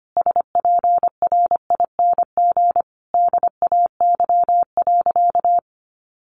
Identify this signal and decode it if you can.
Morse code, heard as a keyed tone, about 25 wpm, 700 Hz. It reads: SPRING DAY.
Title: Sentences from Top 1200 Words 25wpm